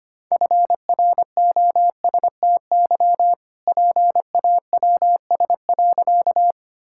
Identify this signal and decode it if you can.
Morse code, heard as a keyed tone, about 25 wpm, 700 Hz. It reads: FROSTY PAWS.
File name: practice / 25wpm / All-Age Appropriate Jokes - 25wpm